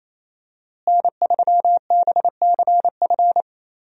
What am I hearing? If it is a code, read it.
Morse code, heard as a keyed tone, about 27 wpm, 700 Hz. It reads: N3BCF
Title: USA Call Signs - 5 Characters - Repeated 3x - 28wpm